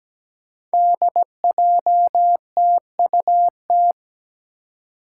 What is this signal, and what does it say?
Morse code, heard as a keyed tone, about 17 wpm, 700 Hz. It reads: DJTUT
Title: Five Challenging Letter-Number Combinations 17wpm